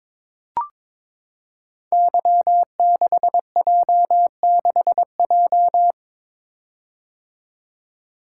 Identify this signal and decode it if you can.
Morse code, heard as a keyed tone, about 22 wpm, 700 Hz. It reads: Y6J6J
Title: Code Groups - 5 Letter:Numbers - 22wpm